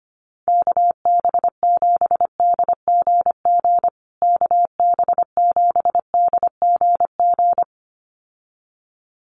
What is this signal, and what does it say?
Morse code, heard as a keyed tone, about 25 wpm, 700 Hz. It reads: KB7DGG KB7DGG
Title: USA Call Signs Repeated 2x 25wpm